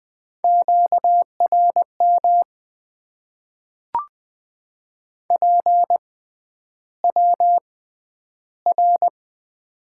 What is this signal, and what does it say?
Morse code, heard as a keyed tone, about 20 wpm, spaced as for 10 wpm, 700 Hz. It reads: QRM PWR
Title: Top 90 QSO Elements - 6x Character Spacing - 20wpm